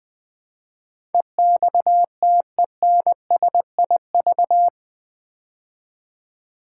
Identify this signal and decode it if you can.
Morse code, heard as a keyed tone, about 20 wpm, 700 Hz. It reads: EXTENSIV
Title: Build A Word - 4 Letters At A Time - 11-Letter Words - 20wpm